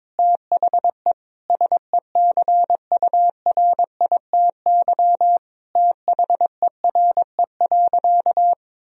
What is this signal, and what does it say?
Morse code, heard as a keyed tone, about 22 wpm, 700 Hz. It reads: THE SECURITY THERE.
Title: Sentences from Top 1000 Words - Review Part 4 - 22wpm